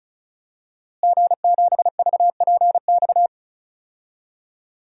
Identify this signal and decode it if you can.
Morse code, heard as a keyed tone, about 36 wpm, 700 Hz. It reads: G7VPX
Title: DXCC-100 Call Signs - 5 Characters - Repeated 3x - 35wpm